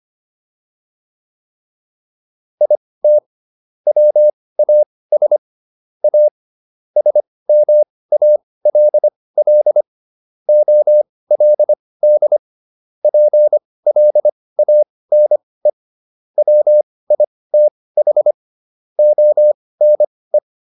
Morse code, heard as a keyed tone, about 25 wpm, spaced as for 18 wpm, 600 Hz. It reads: IT WAS A SMALL OLD PLANE WITH ONE